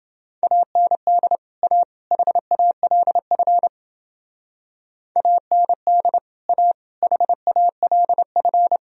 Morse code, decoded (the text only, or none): AND A HALF AND A HALF